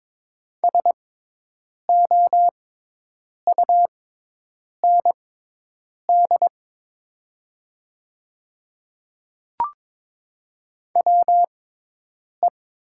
Morse code, heard as a keyed tone, about 22 wpm, spaced as for 8 wpm, 700 Hz. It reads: SOUND WE
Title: 6x Character Spacing - Top 100 Words - 22wpm